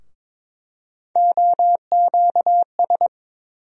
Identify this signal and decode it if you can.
Morse code, heard as a keyed tone, about 22 wpm, 700 Hz. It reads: OQS
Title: USA Call Sign Postfixes - 3 Letters 22wpm